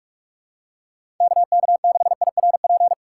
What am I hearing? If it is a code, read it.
Morse code, heard as a keyed tone, about 47 wpm, 700 Hz. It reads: KK6IRP
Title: USA Call Signs - 6 Characters - 4x Character Spacing - 45wpm